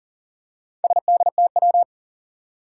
Morse code, heard as a keyed tone, about 40 wpm, 700 Hz. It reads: SDTW
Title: Four Challenging Letter-Number Combinations 40wpm